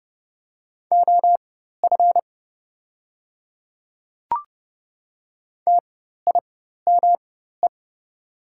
Morse code, heard as a keyed tone, about 30 wpm, spaced as for 14 wpm, 690 Hz. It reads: OF TIME
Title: 4x Character Spacing - Top 100 Words - 30wpm